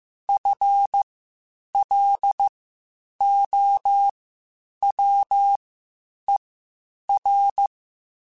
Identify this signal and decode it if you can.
Morse code, heard as a keyed tone, about 15 wpm, spaced as for 8 wpm, 800 Hz. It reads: FLOWER
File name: Spaced / FL Runs sp mp3s / FL Runs sp @13